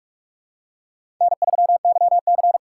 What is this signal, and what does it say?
Morse code, heard as a keyed tone, about 45 wpm, 700 Hz. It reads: N3YC
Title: USA Call Signs - 4 Characters - 2x Character Spacing - 45wpm